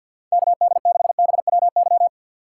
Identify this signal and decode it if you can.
Morse code, heard as a keyed tone, about 51 wpm, 700 Hz. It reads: KD6BWY
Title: USA Call Signs - 3x Character Spacing - 50wpm